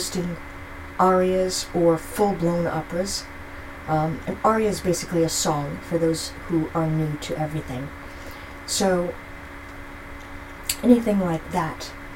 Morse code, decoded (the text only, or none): none